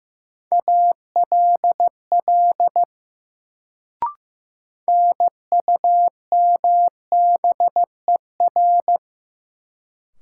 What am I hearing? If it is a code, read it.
Morse code, heard as a keyed tone, about 15 wpm, 700 Hz. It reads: ALL NUMBER
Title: Sets of 2 Words - New Word Drill - 15wpm